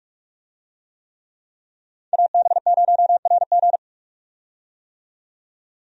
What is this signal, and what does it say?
Morse code, heard as a keyed tone, about 46 wpm, 700 Hz. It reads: AB0RG
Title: USA Call Signs - 5 Characters - 45wpm